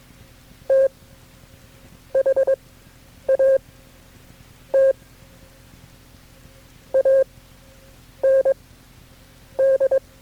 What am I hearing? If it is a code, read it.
Morse code, heard as a keyed tone, about 23 wpm, spaced as for 7 wpm, 540 Hz. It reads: THAT AND
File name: KB6MT-tape3-side1